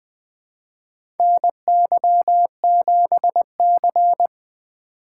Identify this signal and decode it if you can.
Morse code, heard as a keyed tone, about 20 wpm, 700 Hz. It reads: NY7C
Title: USA Call Signs - 4 Characters - 4x Character Spacing - 20wpm